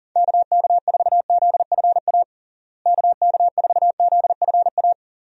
Morse code, heard as a keyed tone, about 40 wpm, 700 Hz. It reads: KK4ZFA KK4ZFA
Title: USA Call Signs - 6 Characters - Repeated 3x - 40wpm